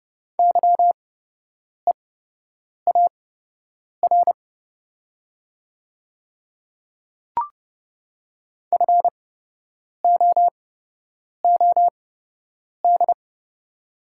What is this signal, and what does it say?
Morse code, heard as a keyed tone, about 30 wpm, spaced as for 8 wpm, 700 Hz. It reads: YEAR FOOD
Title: 8x Character Spacing - Top 300 Words - 30wpm